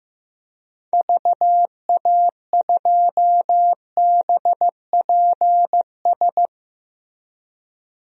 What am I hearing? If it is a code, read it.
Morse code, heard as a keyed tone, about 14 wpm, 700 Hz. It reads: VA2BPS
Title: DXCC-100 Call Signs - 6 Characters - Repeated 2x - 15wpm